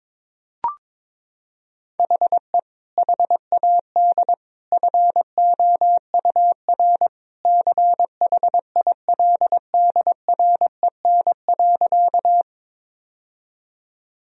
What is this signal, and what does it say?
Morse code, heard as a keyed tone, about 22 wpm, 700 Hz. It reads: HE HAD FOUR CHILDREN.
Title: Sentences from Top 1000 Words - Review Part 1 - 22wpm